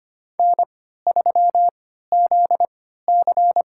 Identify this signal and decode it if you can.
Morse code, heard as a keyed tone, about 25 wpm, spaced as for 14 wpm, 700 Hz. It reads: N3ZC
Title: USA Call Signs - 4 Characters - 3x Character Spacing - 25wpm